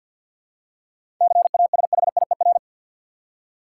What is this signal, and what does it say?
Morse code, heard as a keyed tone, about 51 wpm, 700 Hz. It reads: CASHIER
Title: Practice QSO - Exchange 3 - More Info - 50wpm